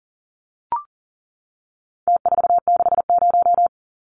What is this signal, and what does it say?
Morse code, heard as a keyed tone, about 40 wpm, 700 Hz. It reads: T460
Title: Numbers - Contest Style - 4-Digit 40wpm